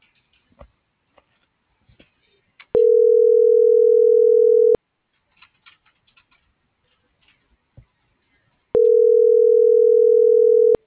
Ringback tone (North America), 440 + 480 Hz.